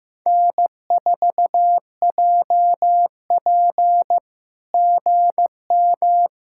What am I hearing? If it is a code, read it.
Morse code, heard as a keyed tone, about 15 wpm, 700 Hz. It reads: N4JP GM
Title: SOTA QSO Practice 15wpm